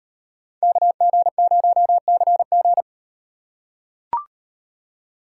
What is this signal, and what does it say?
Morse code, heard as a keyed tone, about 36 wpm, 700 Hz. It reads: KG0CG